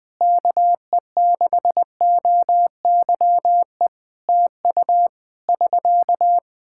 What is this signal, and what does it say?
Morse code, heard as a keyed tone, about 21 wpm, 700 Hz. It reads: KE6OYE TU <SK>
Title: Practice QSO - Exchange 4 - Ending - 20wpm